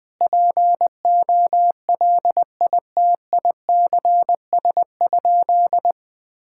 Morse code, heard as a keyed tone, about 20 wpm, 700 Hz. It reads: POLITICS?
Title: Sentences from Top 1000 Words - Review Part 6 - 20wpm